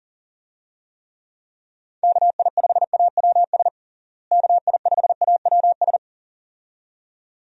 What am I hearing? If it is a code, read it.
Morse code, heard as a keyed tone, about 40 wpm, 700 Hz. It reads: KI5AWS KI5AWS
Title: USA Call Signs - 6 Characters - Repeated 2x - 40wpm